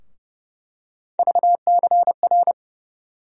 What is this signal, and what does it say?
Morse code, heard as a keyed tone, about 30 wpm, 700 Hz. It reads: VCR
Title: USA Call Sign Postfixes - 3 Letters 30wpm